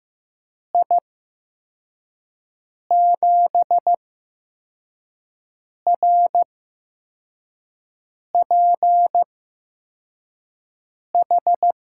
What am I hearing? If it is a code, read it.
Morse code, heard as a keyed tone, about 15 wpm, spaced as for 4 wpm, 700 Hz. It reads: I7RPH
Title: USA Call Signs - 6 Characters - 8x Character Spacing - 15wpm